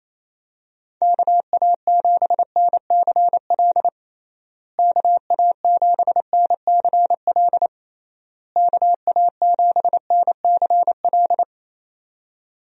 Morse code, heard as a keyed tone, about 27 wpm, 700 Hz. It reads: KA7NCL KA7NCL KA7NCL
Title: USA Call Signs - 6 Characters - Repeated 3x - 28wpm